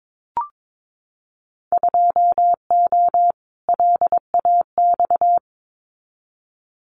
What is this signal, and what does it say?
Morse code, heard as a keyed tone, about 22 wpm, 700 Hz. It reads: 2O LAX
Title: Winter Field Day Exchanges 22wpm